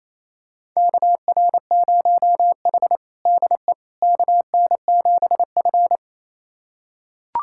Morse code, heard as a keyed tone, about 28 wpm, 700 Hz. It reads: KR0H DE KN7F
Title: USA Callsign DE Callsign - 4 Characters - 28wpm